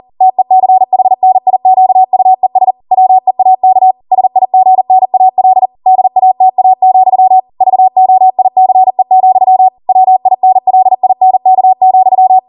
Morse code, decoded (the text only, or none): TECHNIQUES WEAK SIGNAL DATA, VOICE, WINLINK,